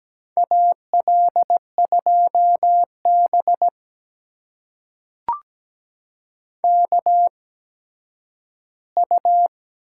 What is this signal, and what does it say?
Morse code, heard as a keyed tone, about 17 wpm, spaced as for 10 wpm, 700 Hz. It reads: AL2B KU
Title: USA Call Signs - 4 Characters - 8x Character Spacing - 17wpm